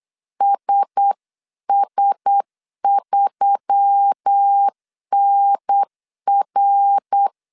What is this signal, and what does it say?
Morse code, heard as a keyed tone, about 8 wpm, 800 Hz. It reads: SS3NR
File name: CW / DET / 1 9 1500C S C 8wpm 5653